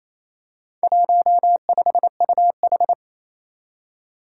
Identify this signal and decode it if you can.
Morse code, heard as a keyed tone, about 28 wpm, 700 Hz. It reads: 15UH
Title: Four Challenging Letter-Number Combinations 28wpm